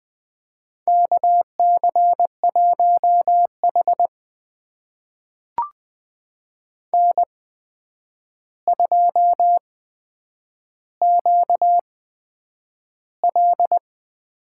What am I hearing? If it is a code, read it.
Morse code, heard as a keyed tone, about 20 wpm, spaced as for 9 wpm, 700 Hz. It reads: KC1H N2QL
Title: USA Call Signs - 4 Characters - 8x Character Spacing - 20wpm